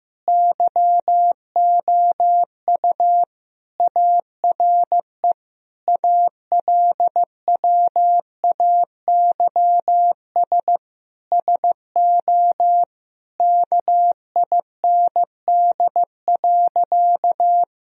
Morse code, heard as a keyed tone, about 16 wpm, 700 Hz. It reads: YOU ARE ALWAYS SO KIND.
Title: Sentences from Top 1000 Words - Review Part 3 - 15wpm